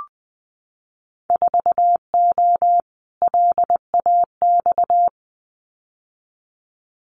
Morse code, heard as a keyed tone, about 20 wpm, 700 Hz. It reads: 4O LAX
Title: Winter Field Day Exchanges Encore 20wpm